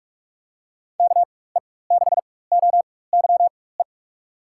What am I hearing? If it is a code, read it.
Morse code, heard as a keyed tone, about 43 wpm, spaced as for 21 wpm, 700 Hz. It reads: KE6OYE
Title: USA Call Signs - 4x Character Spacing - 45wpm